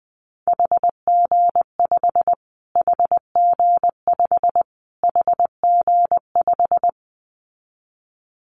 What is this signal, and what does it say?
Morse code, heard as a keyed tone, about 20 wpm, 700 Hz. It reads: HG5 HG5 HG5
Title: Callsign Prefixes for Top 25 DXCC - Repeated 3x - 20wpm